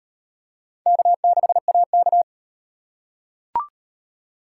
Morse code, heard as a keyed tone, about 35 wpm, 700 Hz. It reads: K6AK